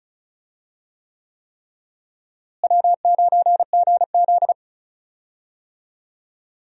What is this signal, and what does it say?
Morse code, heard as a keyed tone, about 36 wpm, 700 Hz. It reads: W9GZ
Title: USA Call Signs - 4 Characters - Repeated 3x - 35wpm